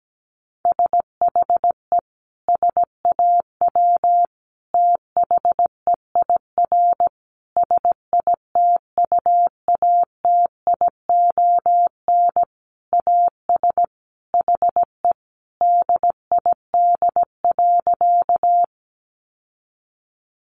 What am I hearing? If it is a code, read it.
Morse code, heard as a keyed tone, about 17 wpm, 700 Hz. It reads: SHE SAW THEIR SITUATION AS HE DID.
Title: Sentences from Top 1200 Words 17wpm